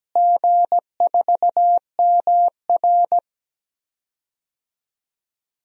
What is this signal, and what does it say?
Morse code, heard as a keyed tone, about 17 wpm, 700 Hz. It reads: G4MR
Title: DXCC-100 Call Signs - 4 Characters - Repeated 2x - 17wpm